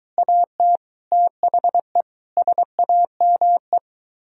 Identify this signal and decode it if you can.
Morse code, heard as a keyed tone, about 23 wpm, 700 Hz. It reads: AT THE SAME